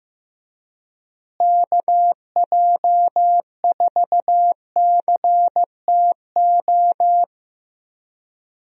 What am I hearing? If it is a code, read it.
Morse code, heard as a keyed tone, about 15 wpm, 700 Hz. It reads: KJ4CTO
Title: USA Call Signs - 6 Characters - Repeated 2x - 15wpm